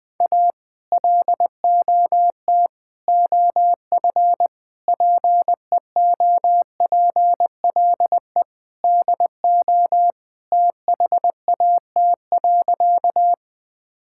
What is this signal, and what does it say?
Morse code, heard as a keyed tone, about 21 wpm, 700 Hz. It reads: A LOT OF PEOPLE DO THAT.
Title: Sentences with 5-Word Phrases - Spoken Beginning - 20wpm